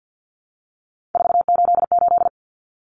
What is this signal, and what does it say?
Morse code, heard as a keyed tone, about 50 wpm, 700 Hz. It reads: 488
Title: Common Cut Numbers - 3-Digit 50wpm